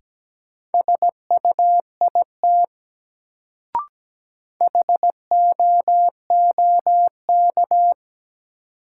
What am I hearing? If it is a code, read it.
Morse code, heard as a keyed tone, about 17 wpm, 700 Hz. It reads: SUIT HOOK
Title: Sentences from Top 1600 Words - New Word Drill Alt - 17wpm